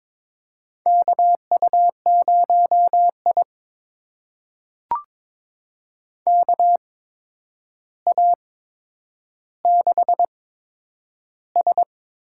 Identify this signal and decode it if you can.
Morse code, heard as a keyed tone, about 22 wpm, spaced as for 9 wpm, 700 Hz. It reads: KU0I KA6S
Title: USA Call Signs - 4 Characters - 8x Character Spacing - 22wpm